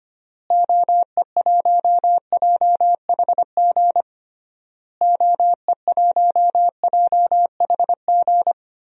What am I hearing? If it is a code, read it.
Morse code, heard as a keyed tone, about 25 wpm, 700 Hz. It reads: OE1JHG OE1JHG
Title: DXCC-100 Call Signs - 6 Characters - Repeated 3x - 25wpm